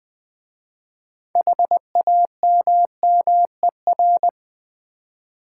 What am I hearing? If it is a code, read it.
Morse code, heard as a keyed tone, about 20 wpm, 700 Hz. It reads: HAMMER
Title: Easy Words with Hints - 20wpm